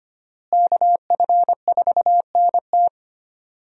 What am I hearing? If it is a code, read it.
Morse code, heard as a keyed tone, about 25 wpm, 700 Hz. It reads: KF4NT
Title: USA Call Signs - 5 Characters - Repeated 3x - 25wpm